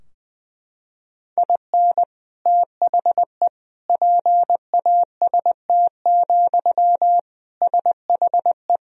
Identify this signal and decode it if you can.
Morse code, heard as a keyed tone, about 20 wpm, 700 Hz. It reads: IN THE PAST, SHE
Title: Sentences from Top 1000 Words - Review Part 3 - 20wpm